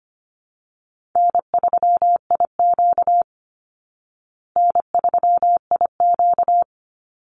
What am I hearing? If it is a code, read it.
Morse code, heard as a keyed tone, about 25 wpm, 700 Hz. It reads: N3IQ N3IQ
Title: Easy Call Signs Repeated 2x - 25wpm